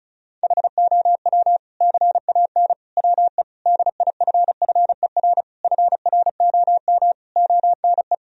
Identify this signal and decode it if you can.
Morse code, heard as a keyed tone, about 34 wpm, 700 Hz. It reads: HOW CAN WE DIFFER FROM ONE
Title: Sentences from Top 1000 Words - Review Part 1 - 35wpm